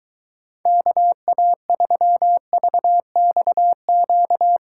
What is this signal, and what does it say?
Morse code, heard as a keyed tone, about 22 wpm, 700 Hz. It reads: KA3VXQ